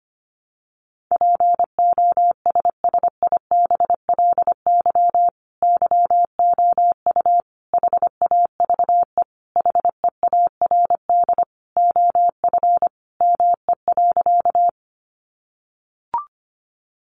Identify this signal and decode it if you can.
Morse code, heard as a keyed tone, about 25 wpm, 700 Hz. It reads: POSSIBLY YOU HAVE HEARD OF ME.
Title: Sentences from Top 1300 Words 25wpm